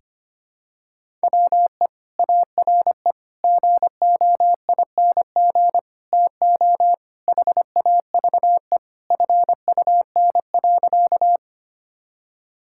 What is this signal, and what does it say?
Morse code, heard as a keyed tone, about 25 wpm, 700 Hz. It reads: WE ARE GOING TO HAVE FUN.
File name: practice / 25wpm / Sentences with 4-Word Phrases - 25wpm